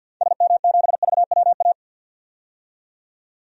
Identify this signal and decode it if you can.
Morse code, heard as a keyed tone, about 50 wpm, 700 Hz. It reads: SM7VWA
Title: DXCC-100 Call Signs - 6 Characters - 50wpm